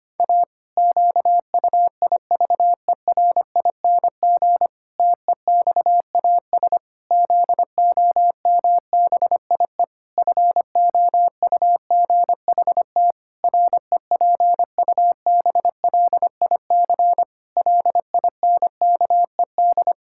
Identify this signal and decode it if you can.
Morse code, heard as a keyed tone, about 25 wpm, 700 Hz. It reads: A QUIVERING TEXAS ZOMBIE FOUGHT REPUBLIC LINKED
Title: Pangram Challenge v2 - 25wpm